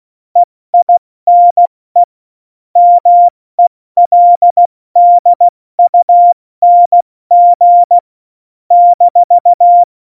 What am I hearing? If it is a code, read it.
Morse code, heard as a keyed tone, about 16 wpm, 700 Hz. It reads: EINE MELDUNG -
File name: Morsekurs / Witze / Witz 22. Januar 2019, 70 F80